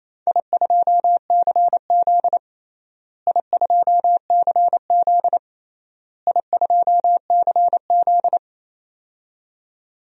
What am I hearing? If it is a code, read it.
Morse code, heard as a keyed tone, about 27 wpm, 700 Hz. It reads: I2CZ I2CZ I2CZ
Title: DXCC-100 Call Signs - 4 Characters - Repeated 3x - 28wpm